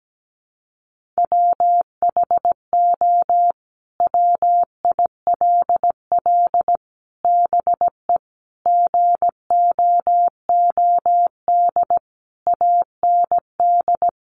Morse code, heard as a keyed tone, about 17 wpm, 700 Hz. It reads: WHO WILL BE GOOD AND
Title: Sentences from Top 1100 Words 17wpm